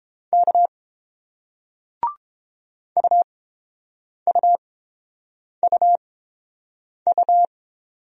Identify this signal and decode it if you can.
Morse code, heard as a keyed tone, about 30 wpm, spaced as for 7 wpm, 700 Hz. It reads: K UUUU